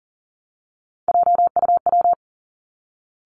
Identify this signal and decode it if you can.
Morse code, heard as a keyed tone, about 40 wpm, 700 Hz. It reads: JUW
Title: Three Challenging Letter-Number Combinations 40wpm